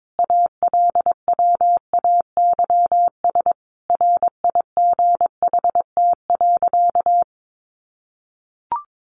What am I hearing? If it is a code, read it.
Morse code, heard as a keyed tone, about 22 wpm, 700 Hz. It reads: ALWAYS RIGHT.
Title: Sentences from Top 1100 Words 22wpm